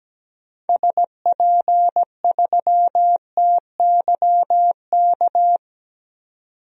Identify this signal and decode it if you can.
Morse code, heard as a keyed tone, about 17 wpm, 700 Hz. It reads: SP3TYK
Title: DXCC-100 Call Signs - 6 Characters - 17wpm